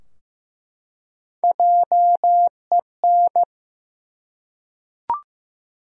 Morse code, heard as a keyed tone, about 15 wpm, 700 Hz. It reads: JEN
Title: USA Common Names - 15wpm